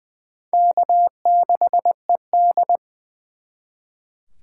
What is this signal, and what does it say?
Morse code, heard as a keyed tone, about 24 wpm, 700 Hz. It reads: K6ED
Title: Speed Racing - USA 1x2 Call Signs 20wpm